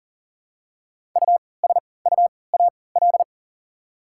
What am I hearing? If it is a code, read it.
Morse code, heard as a keyed tone, about 40 wpm, spaced as for 23 wpm, 690 Hz. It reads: USUAL
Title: 3x Character Spacing - Top 500 Words - 40wpm